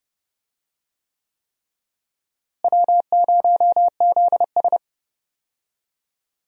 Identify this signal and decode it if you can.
Morse code, heard as a keyed tone, about 30 wpm, 700 Hz. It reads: W0ZS
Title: USA Call Signs - 4 Characters - Repeated 3x - 30wpm